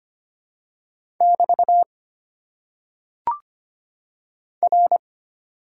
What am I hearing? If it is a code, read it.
Morse code, heard as a keyed tone, about 25 wpm, 690 Hz. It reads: <BT> R
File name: Top 90 QSO Elements - 6x Character Spacing - 25wpm